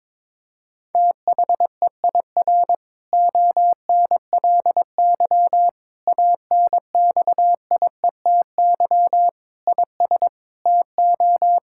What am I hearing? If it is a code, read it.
Morse code, heard as a keyed tone, about 22 wpm, 700 Hz. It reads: THEIR ONLY ANXIETY IS TO
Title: Sentences from Top 1600 Words 22wpm